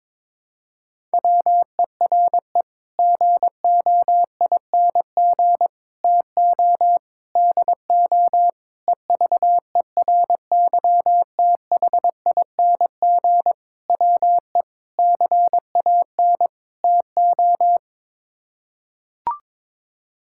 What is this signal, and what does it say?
Morse code, heard as a keyed tone, about 22 wpm, 700 Hz. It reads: WERE GOING TO DO EVERYTHING WE CAN TO
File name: practice / 22wpm / Sets of 8 Words - 22wpm